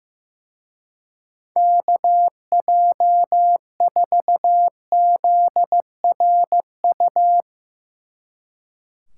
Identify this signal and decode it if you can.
Morse code, heard as a keyed tone, about 15 wpm, 700 Hz. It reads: KJ4ZRU
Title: Speed Racing - USA 2x3 Call Signs 15wpm